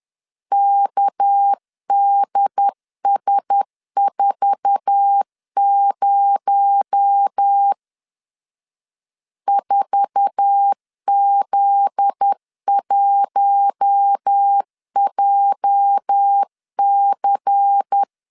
Morse code, heard as a keyed tone, about 10 wpm, 800 Hz. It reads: KDS40 4Z1JC
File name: LEZIONI / 8 YJ45 / 8 YJ4 10wpm 2633